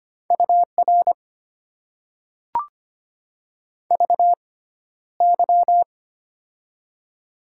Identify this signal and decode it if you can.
Morse code, heard as a keyed tone, about 25 wpm, spaced as for 13 wpm, 700 Hz. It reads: UR VY
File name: Top 90 QSO Elements - 6x Character Spacing - 25wpm